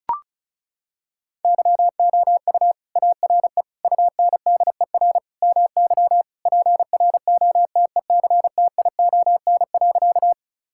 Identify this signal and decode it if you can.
Morse code, heard as a keyed tone, about 34 wpm, 700 Hz. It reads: YOU ARE UNDER MY PROTECTION.
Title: Sentences from Top 1600 Words 35wpm